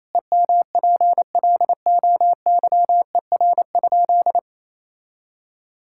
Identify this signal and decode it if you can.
Morse code, heard as a keyed tone, about 28 wpm, 700 Hz. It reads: EMPLOYER?
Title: Sentences from Top 1600 Words 28wpm